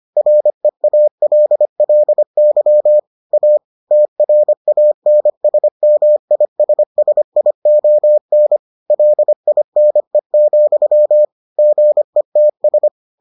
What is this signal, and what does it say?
Morse code, heard as a keyed tone, about 25 wpm, 600 Hz. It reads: REALLY A TRANSMISSION LINE, GETS